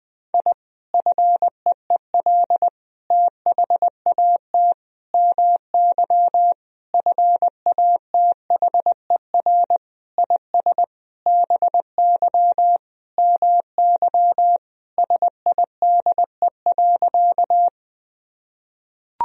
Morse code, heard as a keyed tone, about 20 wpm, 700 Hz. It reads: I FEEL THAT MY FATHER IS BY MY SIDE.
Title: Sentences from Top 1000 Words - Review Part 1 - 20wpm